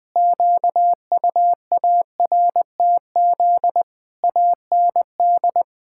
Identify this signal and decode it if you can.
Morse code, heard as a keyed tone, about 21 wpm, 700 Hz. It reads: QUARTZ AND
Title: Pangram Challenge v2 - 20wpm